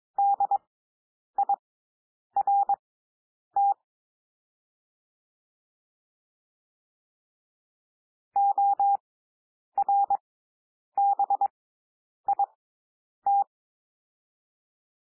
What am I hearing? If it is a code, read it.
Morse code, heard as a keyed tone, about 22 wpm, spaced as for 9 wpm, 800 Hz. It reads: DIRT ORBIT
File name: SOUND WORDS 22 wpm